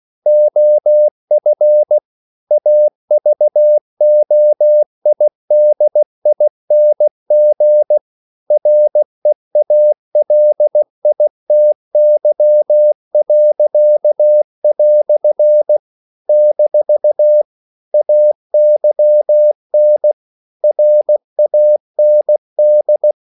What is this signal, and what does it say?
Morse code, heard as a keyed tone, about 16 wpm, 600 Hz. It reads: OF AVOIDING REALITY." - AYN RAND
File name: CW / Practice / EnglishQuotations0116